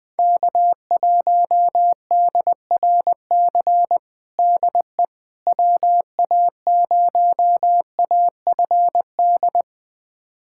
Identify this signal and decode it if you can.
Morse code, heard as a keyed tone, about 20 wpm, 700 Hz. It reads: K1DRC DE WA0AFD
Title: USA Callsign DE Callsign 20wpm